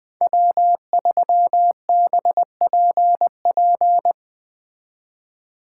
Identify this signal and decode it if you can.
Morse code, heard as a keyed tone, about 20 wpm, 700 Hz. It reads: W3BPP
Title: Speed Racing - USA Call Signs 17wpm